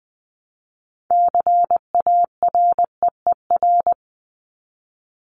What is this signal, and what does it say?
Morse code, heard as a keyed tone, about 20 wpm, 700 Hz. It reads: CAREER
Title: Top 751-1000 Words 20wpm